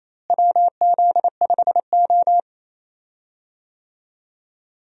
Morse code, heard as a keyed tone, about 27 wpm, 700 Hz. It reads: WZ5O
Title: USA Call Signs - 4 Characters - Repeated 3x - 28wpm